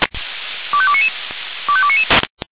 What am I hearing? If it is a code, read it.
Five-tone selective call (ZVEI), heard as quick runs of five tones, tones of about 70 ms. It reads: 25299 25299